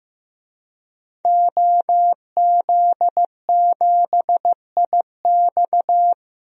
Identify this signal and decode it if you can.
Morse code, heard as a keyed tone, about 15 wpm, 700 Hz. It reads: OZ7IX
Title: DXCC-100 Call Signs - 5 Characters - Repeated 3x - 15wpm